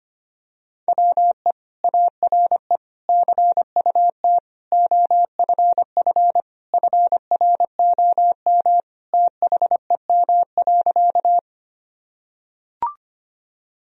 Morse code, heard as a keyed tone, about 25 wpm, 700 Hz. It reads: WE ARE CUT OFF FROM THEM.
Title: Sentences from Top 1000 Words - Review Part 1 - 25wpm